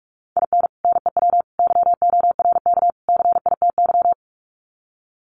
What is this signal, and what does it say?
Morse code, heard as a keyed tone, about 44 wpm, 700 Hz. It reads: IN NEW YORK CITY
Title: Sets of 4 Words 45wpm